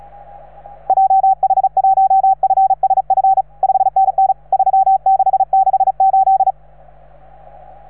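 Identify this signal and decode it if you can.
Morse code, heard as a keyed tone, about 35 wpm, 770 Hz. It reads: JH1FSF 5NN 3668
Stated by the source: Morse from station 3V9A.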